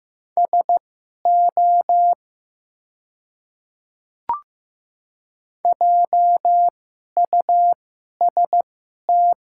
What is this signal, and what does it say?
Morse code, heard as a keyed tone, about 15 wpm, spaced as for 11 wpm, 700 Hz. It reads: SO JUST
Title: 2x Character Spacing - Top 500 Words - 15wpm